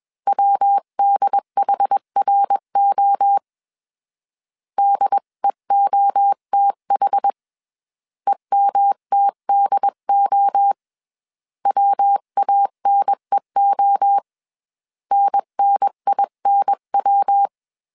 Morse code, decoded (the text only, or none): WDHRO DEOTH EMTDO WANEO NNINW